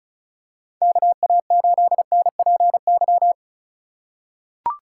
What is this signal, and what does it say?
Morse code, heard as a keyed tone, about 34 wpm, 700 Hz. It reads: KA8NPY
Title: USA Call Signs - 4x Character Spacing - 35wpm